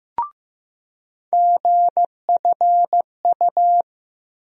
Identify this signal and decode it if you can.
Morse code, heard as a keyed tone, about 15 wpm, 700 Hz. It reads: GFU
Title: USA Call Sign Postfixes - 3 Letters 15wpm